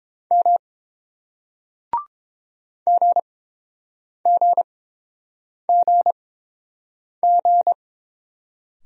Morse code, heard as a keyed tone, about 32 wpm, spaced as for 7 wpm, 700 Hz. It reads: M GGGG